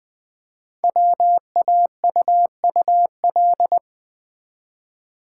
Morse code, heard as a keyed tone, about 20 wpm, 700 Hz. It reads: WAUUL
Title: Code Groups - 5 Letters - 20wpm